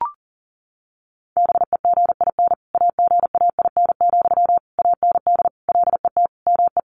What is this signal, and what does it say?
Morse code, heard as a keyed tone, about 40 wpm, 700 Hz. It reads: BEGIN AGAIN, AND LET ME
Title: Sentences from Top 800 Words 40wpm